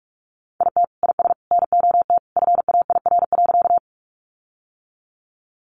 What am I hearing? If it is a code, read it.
Morse code, heard as a keyed tone, about 44 wpm, 700 Hz. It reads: IT IS NOT FAIR.
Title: Sentences from Top 1300 Words 45wpm